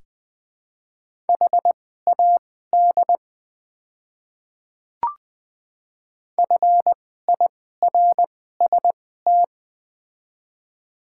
Morse code, heard as a keyed tone, about 20 wpm, spaced as for 15 wpm, 700 Hz. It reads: HAD FIRST
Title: 2x Character Spacing - Top 300 Words - 20wpm